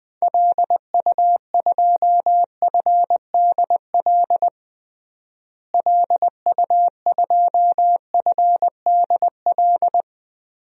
Morse code, heard as a keyed tone, about 20 wpm, 700 Hz. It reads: LU2FDL LU2FDL
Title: DXCC-100 Call Signs - 6 Characters - Repeated 3x - 20wpm